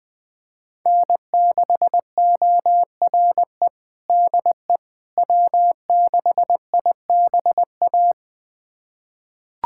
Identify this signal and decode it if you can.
Morse code, heard as a keyed tone, about 20 wpm, 700 Hz. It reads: N6ORE DE W6IBA
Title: USA Callsign DE Callsign - 5 Characters - 20wpm